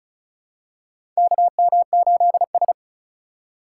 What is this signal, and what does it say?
Morse code, heard as a keyed tone, about 36 wpm, 700 Hz. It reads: KM8S
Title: USA Call Signs - 4 Characters - 4x Character Spacing - 35wpm